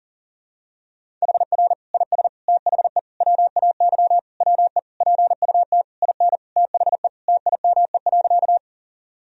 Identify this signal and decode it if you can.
Morse code, heard as a keyed tone, about 40 wpm, 700 Hz. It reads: HR IS THE WAY WE PUT IN THE TIME.
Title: Sentences from Top 200 Words and Abbreviations 40wpm